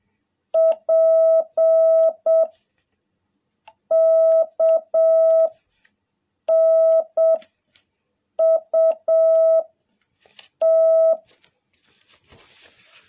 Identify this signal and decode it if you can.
Morse code, heard as a keyed tone, about 7 wpm, 660 Hz. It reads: P K N U T
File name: jh8cbh / 2021cw